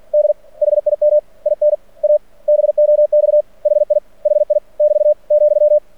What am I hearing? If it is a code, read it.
Morse code, heard as a keyed tone, about 49 wpm, 600 Hz. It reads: D HIM IN A BOX HI HI <BT> ,
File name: archived / CRSnet